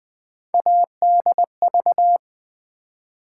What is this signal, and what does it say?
Morse code, heard as a keyed tone, about 20 wpm, 700 Hz. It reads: ADV
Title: Build A Word - 3 Letters At A Time - 9-Letter Words - 20wpm